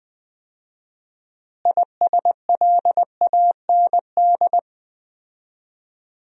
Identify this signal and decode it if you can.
Morse code, heard as a keyed tone, about 20 wpm, 700 Hz. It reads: ISLAND
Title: 4x Word-Spacing - Sentences from Top 500 Words - 20wpm